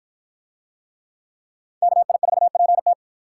Morse code, heard as a keyed tone, about 53 wpm, 700 Hz. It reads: KI4PT